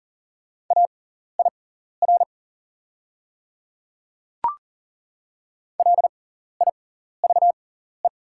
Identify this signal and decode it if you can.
Morse code, heard as a keyed tone, about 40 wpm, spaced as for 14 wpm, 700 Hz. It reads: AIR LIVE
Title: 6x Character Spacing - Top 200 Words - 40wpm